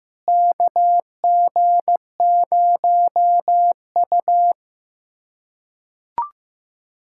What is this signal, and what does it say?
Morse code, heard as a keyed tone, about 15 wpm, 700 Hz. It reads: KG0U
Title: USA Callsign DE Callsign - 4 Characters - 15wpm